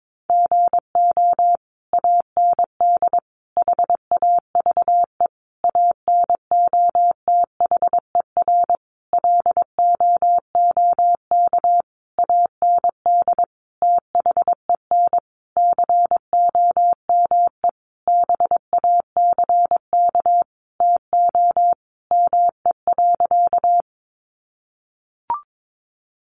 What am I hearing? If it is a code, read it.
Morse code, heard as a keyed tone, about 22 wpm, 700 Hz. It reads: GO AND HAVE ANOTHER LOOK AND THEN COME BACK TO ME.
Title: Sentences from Top 600 Words 22wpm